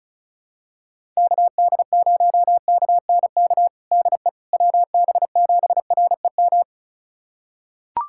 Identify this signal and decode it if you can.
Morse code, heard as a keyed tone, about 34 wpm, 700 Hz. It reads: KD0KNK DE WB7REM
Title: USA Callsign DE Callsign - 6 Characters - 35wpm